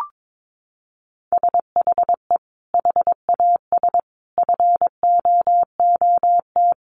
Morse code, heard as a keyed tone, about 22 wpm, 700 Hz. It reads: SHE HAS FOOT
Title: Sentences from Top 1500 Words 22wpm